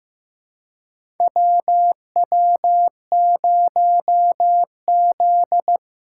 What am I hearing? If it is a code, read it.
Morse code, heard as a keyed tone, about 16 wpm, 700 Hz. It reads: WW0Z
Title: USA Call Signs - 3x Character Spacing - 15wpm